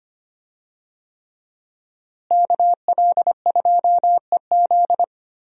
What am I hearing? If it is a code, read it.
Morse code, heard as a keyed tone, about 25 wpm, 700 Hz. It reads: KL2EZ
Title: USA Call Signs - 5 Characters - Repeated 3x - 25wpm